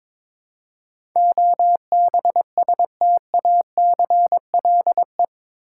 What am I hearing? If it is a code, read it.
Morse code, heard as a keyed tone, about 22 wpm, 700 Hz. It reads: OBSTACLE
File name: practice / 22wpm / Eight Letter Words - 22wpm